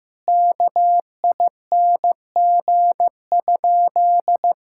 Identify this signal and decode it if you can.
Morse code, heard as a keyed tone, about 15 wpm, 700 Hz. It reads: KING?
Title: Sentences from Top 1000 Words - Review Part 5 - 15wpm